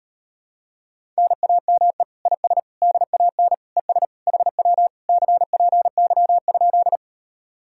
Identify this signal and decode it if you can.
Morse code, heard as a keyed tone, about 38 wpm, 700 Hz. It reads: NAME IS DAN ES HW CPY?